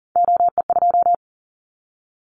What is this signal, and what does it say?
Morse code, heard as a keyed tone, about 40 wpm, 700 Hz. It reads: OE2
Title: Callsign Prefixes for Top 100 DXCC - 40wpm